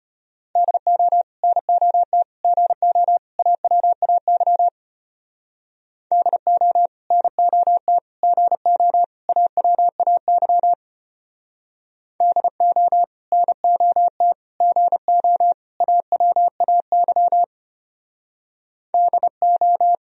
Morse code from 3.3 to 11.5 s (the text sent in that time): AWAY DO NOT GO AWAY